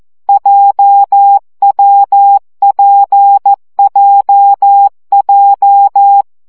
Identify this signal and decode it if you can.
Morse code, heard as a keyed tone, about 14 wpm, 800 Hz. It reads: JWPJJ
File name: CW gyakorlas w p j 12 baud